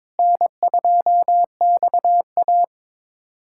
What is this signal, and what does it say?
Morse code, heard as a keyed tone, about 22 wpm, 700 Hz. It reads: N2XA
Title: USA Callsign DE Callsign - 4 Characters - 22wpm